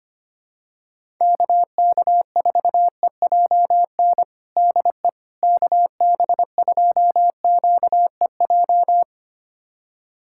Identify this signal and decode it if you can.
Morse code, heard as a keyed tone, about 25 wpm, 700 Hz. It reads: KK4EJN DE KB2QEJ
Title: USA Callsign DE Callsign 25wpm